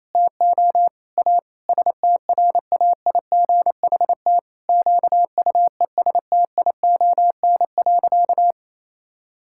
Morse code, heard as a keyed tone, about 28 wpm, 700 Hz. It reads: TO A STRAIGHT QUESTION.
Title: Sentences from Top 1000 Words - Review Part 6 - 28wpm